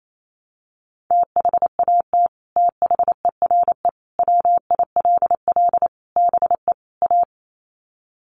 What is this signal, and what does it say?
Morse code, heard as a keyed tone, about 28 wpm, 700 Hz. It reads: THAT THERE WILL BE A
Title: Sets of 5 Words Encore2 28wpm